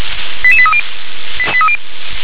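Five-tone selective call (ZVEI), heard as quick runs of five tones, tones of about 70 ms. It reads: 88830
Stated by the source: ZVEI five-tone selective call 88824 received by SDR.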